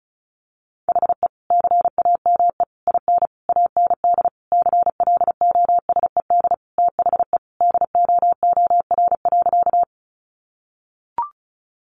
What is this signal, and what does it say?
Morse code, heard as a keyed tone, about 34 wpm, 700 Hz. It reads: HE CAME IN AND CLOSED THE DOOR.
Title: Sentences from Top 1400 Words 35wpm